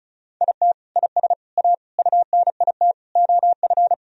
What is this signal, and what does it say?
Morse code, heard as a keyed tone, about 34 wpm, 700 Hz. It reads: IT IS A UNIT OF
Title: Sentences from Top 1000 Words - Review Part 2 - 35wpm